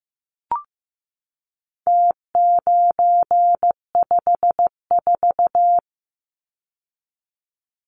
Morse code, heard as a keyed tone, about 15 wpm, 700 Hz. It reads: T954
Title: Numbers - Contest Style - 4-Digit 15wpm